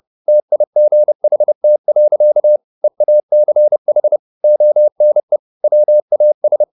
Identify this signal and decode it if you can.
Morse code, heard as a keyed tone, about 30 wpm, 600 Hz. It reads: TIGHT. EACH ONE WAS
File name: the old man and the sea / the old man and the sea 1 30wpm